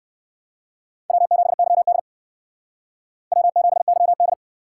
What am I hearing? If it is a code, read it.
Morse code, heard as a keyed tone, about 59 wpm, 700 Hz. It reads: W7YD W7YD